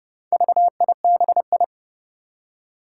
Morse code, heard as a keyed tone, about 30 wpm, 700 Hz. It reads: VIBI
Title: Four Challenging Letter-Number Combinations 30wpm